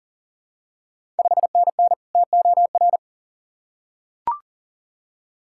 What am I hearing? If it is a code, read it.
Morse code, heard as a keyed tone, about 38 wpm, 700 Hz. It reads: 5NN TOR
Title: Ontario QSO Party Exchange 40wpm